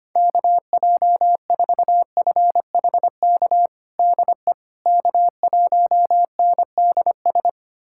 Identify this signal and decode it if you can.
Morse code, heard as a keyed tone, about 25 wpm, 700 Hz. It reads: KJ4FHK DE K1NDS
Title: USA Callsign DE Callsign 25wpm